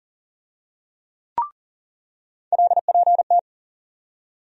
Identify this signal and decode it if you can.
Morse code, heard as a keyed tone, about 40 wpm, 700 Hz. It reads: LPT
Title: USA Call Sign Postfixes - 3 Letters 40wpm